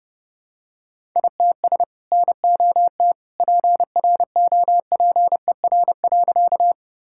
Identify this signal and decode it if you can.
Morse code, heard as a keyed tone, about 30 wpm, 700 Hz. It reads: ITS NOT PROPER.
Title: Sentences from Top 1000 Words - Review Part 5 - 30wpm